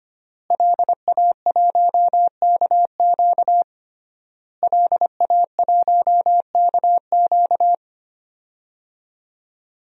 Morse code, heard as a keyed tone, about 25 wpm, 700 Hz. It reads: LA1KQ LA1KQ
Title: DXCC-100 Call Signs - 5 Characters - Repeated 3x - 25wpm